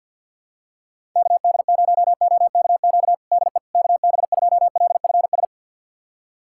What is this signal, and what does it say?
Morse code, heard as a keyed tone, about 51 wpm, 700 Hz. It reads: KD0OKQ DE KB2LFS
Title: USA Callsign DE Callsign - 6 Characters - 50wpm